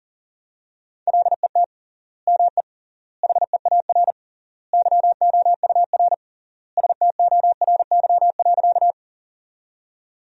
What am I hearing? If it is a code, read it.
Morse code, heard as a keyed tone, about 40 wpm, 700 Hz. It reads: LET ME HEAR YOUR STORY.
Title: 3x Word-Spacing - Sentences from Top 300 Words - 40wpm